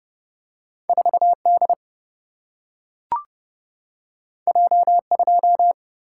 Morse code, heard as a keyed tone, about 30 wpm, 700 Hz. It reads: 4 D J 2
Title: Single-Letter Single-Number 30wpm